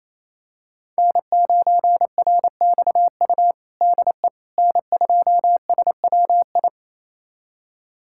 Morse code, heard as a keyed tone, about 28 wpm, 700 Hz. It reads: N9RXU DE N2SWI
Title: USA Callsign DE Callsign - 5 Characters - 28wpm